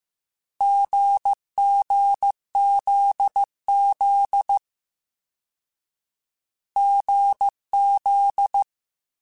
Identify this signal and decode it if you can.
Morse code, heard as a keyed tone, about 15 wpm, 800 Hz. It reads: GGZZ GZ
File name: Short Runs / Short Runs-mp3s / GZ GZ7 @13